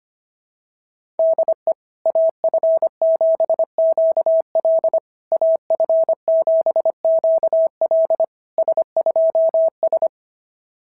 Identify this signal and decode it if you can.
Morse code, heard as a keyed tone, about 24 wpm, 650 Hz. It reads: DE AF7QL AF7QL S2S
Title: SOTA QSO Practice 25wpm